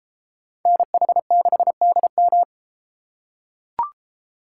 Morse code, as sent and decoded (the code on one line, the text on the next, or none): -. .... -.... -.. --
NH6DM